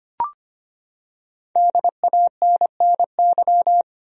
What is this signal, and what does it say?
Morse code, heard as a keyed tone, about 25 wpm, 700 Hz. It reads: DANNY
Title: USA Common Names - 25wpm